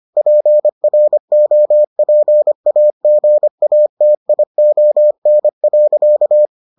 Morse code, heard as a keyed tone, about 25 wpm, 600 Hz. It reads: PROPAGATION.